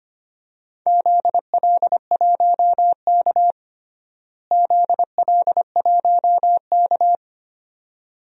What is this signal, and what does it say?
Morse code, heard as a keyed tone, about 25 wpm, 700 Hz. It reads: ZL1K ZL1K
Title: DXCC-100 Call Signs - 4 Characters - Repeated 2x - 25wpm